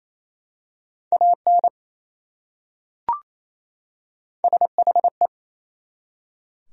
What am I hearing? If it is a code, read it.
Morse code, heard as a keyed tone, about 28 wpm, 700 Hz. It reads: AN SHE
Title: Top 100 Words 28wpm